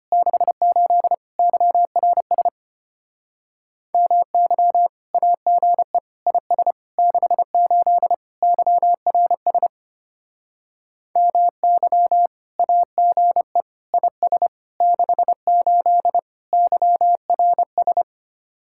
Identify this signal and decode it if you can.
Morse code, heard as a keyed tone, about 31 wpm, 700 Hz. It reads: 68 YRS MY AGE IS 68 YRS MY AGE IS 68 YRS
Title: Speed-Racing Common QSO Phrases - 25wpm